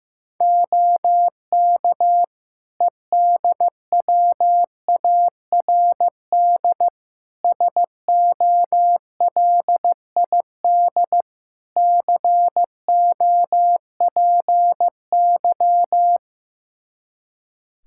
Morse code, heard as a keyed tone, about 15 wpm, 700 Hz. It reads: OK EDWARD SOLID COPY
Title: Practice QSO - Exchange 3 - More Info - 15wpm